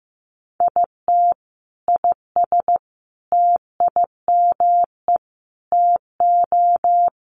Morse code, heard as a keyed tone, about 15 wpm, 700 Hz. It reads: IT IS TIME TO
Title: Sentences from Top 1200 Words 15wpm